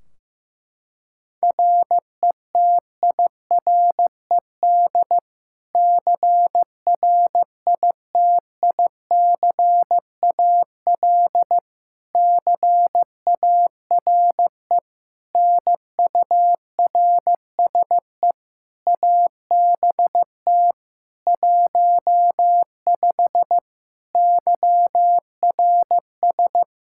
Morse code, heard as a keyed tone, about 15 wpm, 700 Hz. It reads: RETIRED CRITICAL CARE NURSE ABT 15 YRS
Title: Practice QSO - Exchange 3 - More Info - 15wpm